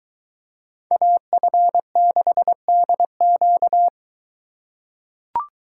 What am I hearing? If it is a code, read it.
Morse code, heard as a keyed tone, about 23 wpm, 700 Hz. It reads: AF6DQ